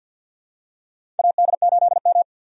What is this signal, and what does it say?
Morse code, heard as a keyed tone, about 48 wpm, 700 Hz. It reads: AD8M
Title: USA Call Signs - 4 Characters - 50wpm